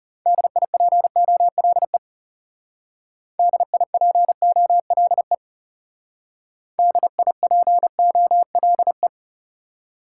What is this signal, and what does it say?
Morse code, heard as a keyed tone, about 37 wpm, 700 Hz. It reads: DIPOLE DIPOLE DIPOLE
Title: Speed-Racing Top 90 QSO Elements 30wpm